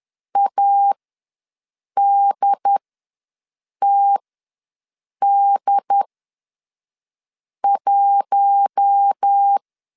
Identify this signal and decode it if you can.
Morse code, heard as a keyed tone, about 10 wpm, spaced as for 5 wpm, 800 Hz. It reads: ADTD1
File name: LEZIONI / 5 LBVC / 5 LBV ws 10wpm 3645